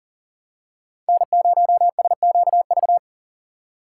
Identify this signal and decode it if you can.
Morse code, heard as a keyed tone, about 40 wpm, 700 Hz. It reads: N0SQV
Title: USA Call Signs - 2x Character Spacing - 40wpm